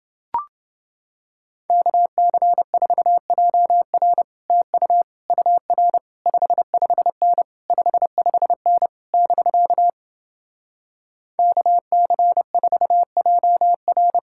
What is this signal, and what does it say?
Morse code, heard as a keyed tone, about 30 wpm, 700 Hz. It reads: KC4JR TU UR 55N 55N <BK> KC4JR